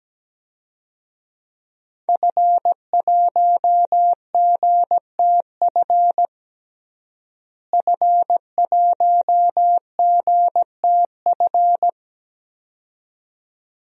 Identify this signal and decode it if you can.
Morse code, heard as a keyed tone, about 17 wpm, 700 Hz. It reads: F1GTF F1GTF
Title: DXCC-100 Call Signs - 5 Characters - Repeated 2x - 17wpm